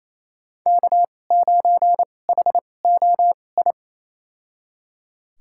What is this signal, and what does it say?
Morse code, heard as a keyed tone, about 28 wpm, spaced as for 20 wpm, 700 Hz. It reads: K9HOI
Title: USA Call Signs - 5 Characters - 2x Character Spacing - 28wpm